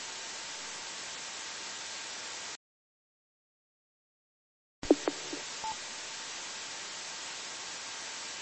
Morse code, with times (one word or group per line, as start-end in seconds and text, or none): none